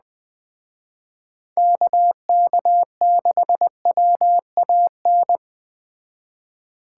Morse code, heard as a keyed tone, about 21 wpm, 700 Hz. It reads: KK6WAN